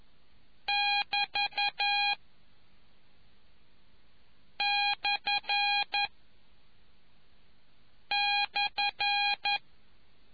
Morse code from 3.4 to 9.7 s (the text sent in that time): / /